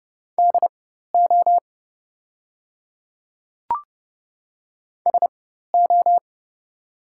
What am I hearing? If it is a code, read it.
Morse code, heard as a keyed tone, about 30 wpm, spaced as for 14 wpm, 700 Hz. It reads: DO SO
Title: 4x Character Spacing - Top 100 Words - 30wpm